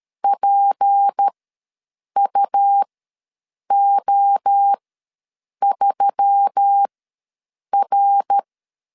Morse code, heard as a keyed tone, about 13 wpm, spaced as for 7 wpm, 800 Hz. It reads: PUO3R
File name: LEZIONI / 6 PKX0 / 6 P ws 12wpm 3117